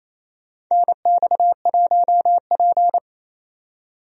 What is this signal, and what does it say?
Morse code, heard as a keyed tone, about 28 wpm, 700 Hz. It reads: NX1P
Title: USA Call Signs - 4 Characters - 28wpm